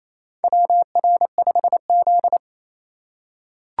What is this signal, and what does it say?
Morse code, heard as a keyed tone, about 28 wpm, 700 Hz. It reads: WR5Z
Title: USA Call Signs - 4 Characters - 2x Character Spacing - 28wpm